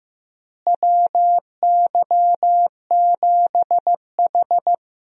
Morse code, heard as a keyed tone, about 15 wpm, 700 Hz. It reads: WY7H
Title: USA Call Signs - 4 Characters - Repeated 3x - 15wpm